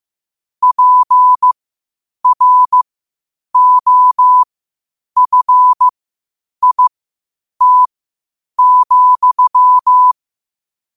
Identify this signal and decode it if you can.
Morse code, heard as a keyed tone, about 15 wpm, spaced as for 8 wpm, 1000 Hz. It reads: PROFIT,
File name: qotdfiles / 8wpm